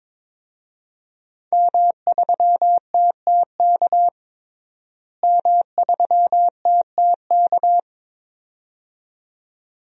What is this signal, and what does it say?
Morse code, heard as a keyed tone, about 22 wpm, 700 Hz. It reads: M3TTK M3TTK
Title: DXCC-100 Call Signs - 5 Characters - Repeated 2x - 22wpm